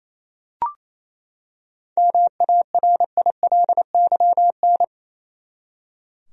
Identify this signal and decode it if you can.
Morse code, heard as a keyed tone, about 28 wpm, 700 Hz. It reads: MARILYN
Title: USA Common Names - 28wpm